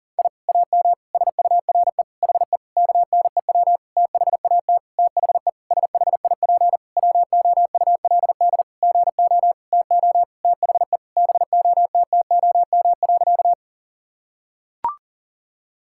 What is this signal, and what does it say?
Morse code, heard as a keyed tone, about 40 wpm, 700 Hz. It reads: I AM SURE HE KNEW THAT THE SHIP WOULD GO TO THE BOTTOM.
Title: Sentences from Top 1000 Words - Review Part 6 - 40wpm